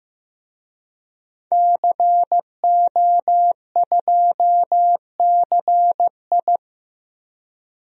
Morse code, heard as a keyed tone, about 15 wpm, 700 Hz. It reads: CO2CI
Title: DXCC-100 Call Signs - 5 Characters - 15wpm